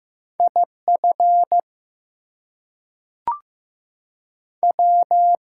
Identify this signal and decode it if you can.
Morse code, heard as a keyed tone, about 15 wpm, 700 Hz. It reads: IF W
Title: Two Letter Words - 15wpm